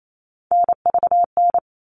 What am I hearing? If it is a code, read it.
Morse code, heard as a keyed tone, about 28 wpm, 700 Hz. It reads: NVN
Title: All Cut Numbers - 3-Digit 28wpm